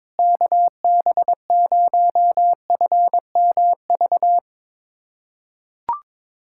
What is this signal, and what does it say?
Morse code, heard as a keyed tone, about 22 wpm, 700 Hz. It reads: KB0FMV
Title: USA Callsign DE Callsign 22wpm